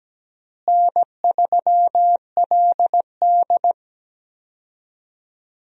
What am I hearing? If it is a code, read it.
Morse code, heard as a keyed tone, about 17 wpm, 700 Hz. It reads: N3LD
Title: USA Call Signs - 4 Characters - Repeated 2x - 17wpm